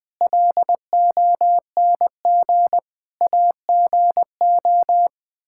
Morse code, heard as a keyed tone, about 20 wpm, 700 Hz. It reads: LONG AGO
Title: Sets of 8 Words - 20wpm